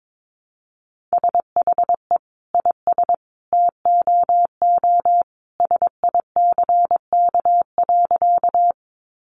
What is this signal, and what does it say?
Morse code, heard as a keyed tone, about 22 wpm, 700 Hz. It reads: SHE IS TOO SICK.
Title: Easy Phrases 22wpm